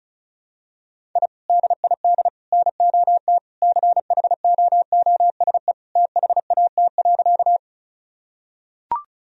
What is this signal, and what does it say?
Morse code, heard as a keyed tone, about 36 wpm, 700 Hz. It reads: I DID NOT CHOOSE THAT.
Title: Sentences from Top 1000 Words - Review Part 4 - 35wpm